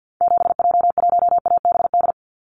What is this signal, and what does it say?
Morse code, heard as a keyed tone, about 50 wpm, 700 Hz. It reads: 7J1ABD
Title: Call Signs - 50wpm